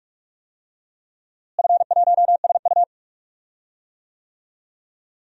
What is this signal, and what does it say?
Morse code, heard as a keyed tone, about 46 wpm, 700 Hz. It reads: F1SU
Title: DXCC-100 Call Signs - 4 Characters - Repeated 2x - 45wpm